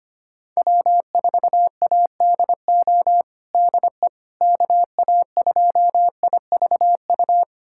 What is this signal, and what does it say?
Morse code, heard as a keyed tone, about 25 wpm, 700 Hz. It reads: W4ADO DE KA2IVU
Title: USA Callsign DE Callsign 25wpm